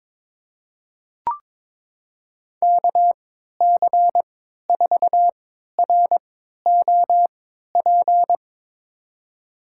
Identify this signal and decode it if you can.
Morse code, heard as a keyed tone, about 21 wpm, spaced as for 12 wpm, 700 Hz. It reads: KC4ROP
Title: USA Call Signs - 3x Character Spacing - 22wpm